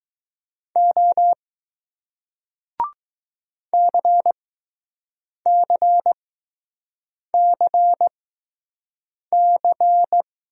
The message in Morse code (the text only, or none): O CCCC